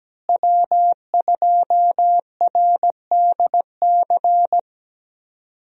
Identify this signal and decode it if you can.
Morse code, heard as a keyed tone, about 17 wpm, 700 Hz. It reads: W2RDC
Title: USA Call Signs - 5 Characters - Repeated 3x - 17wpm